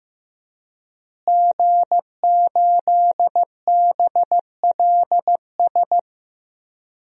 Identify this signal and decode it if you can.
Morse code, heard as a keyed tone, about 15 wpm, 700 Hz. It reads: G8BLS
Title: DXCC-100 Call Signs - 5 Characters - Repeated 2x - 15wpm